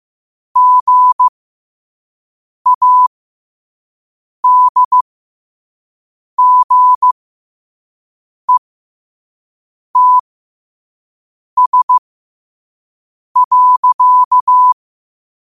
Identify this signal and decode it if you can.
Morse code, heard as a keyed tone, about 15 wpm, spaced as for 5 wpm, 1000 Hz. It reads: GADGETS.
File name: qotdfiles / 5wpm